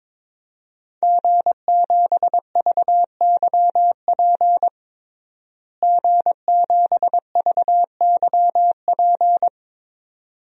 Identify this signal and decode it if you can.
Morse code, heard as a keyed tone, about 22 wpm, 700 Hz. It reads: G7VYP G7VYP
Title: DXCC-100 Call Signs - 5 Characters - Repeated 2x - 22wpm